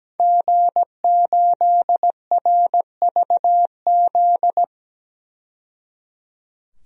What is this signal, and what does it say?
Morse code, heard as a keyed tone, about 17 wpm, 700 Hz. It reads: G8RVZ
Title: DXCC-100 Call Signs - 5 Characters - 17wpm